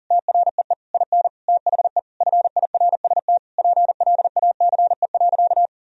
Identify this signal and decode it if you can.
Morse code, heard as a keyed tone, about 40 wpm, 700 Hz. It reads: TREE IN THE FIRST PLACE.
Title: Sentences from Top 1000 Words - Review Part 1 - 40wpm